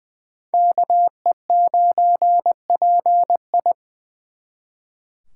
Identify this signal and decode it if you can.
Morse code, heard as a keyed tone, about 24 wpm, 700 Hz. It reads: KE9PI
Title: Speed Racing - USA 2x2 Call Signs 20wpm